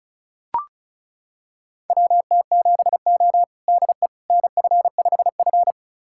Morse code, heard as a keyed tone, about 36 wpm, 700 Hz. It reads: WT7O DE NF5F
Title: USA Callsign DE Callsign - 4 Characters - 35wpm